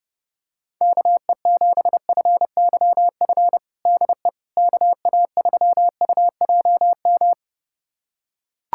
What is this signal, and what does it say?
Morse code, heard as a keyed tone, about 30 wpm, 700 Hz. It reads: KE7FYF DE KA3UJM
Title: USA Callsign DE Callsign - 6 Characters - 30wpm